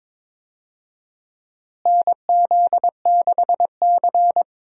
Morse code, heard as a keyed tone, about 22 wpm, 700 Hz. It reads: NZ6C
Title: USA Call Signs - 4 Characters - Repeated 2x - 22wpm